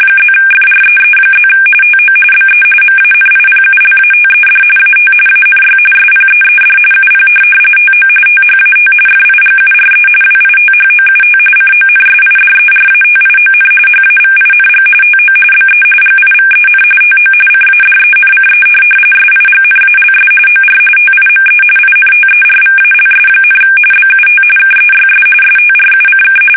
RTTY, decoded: +FREE+ TO BE MANIPULATED AT ANY SPEED UP TO THAT FIGURE WITHOUT CONSIDERATION HAVING TO BE GIVEN TO CADENCE SIGNALS OR TIME INTERVALS, BUT KEYS STRUCK

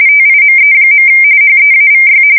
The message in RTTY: TIME. HEAVY

